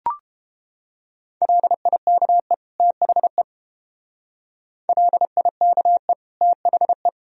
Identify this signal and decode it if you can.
Morse code, heard as a keyed tone, about 30 wpm, 700 Hz. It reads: LIKE THE LIKE THE